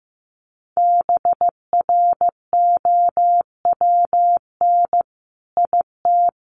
Morse code, heard as a keyed tone, about 15 wpm, 700 Hz. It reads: BROWN IT
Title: Sentences from Top 1200 Words 15wpm